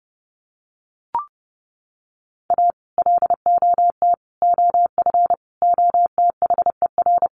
Morse code, heard as a keyed tone, about 30 wpm, 700 Hz. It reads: A LOT OF OTHER
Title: Sets of 5 Words Encore2 30wpm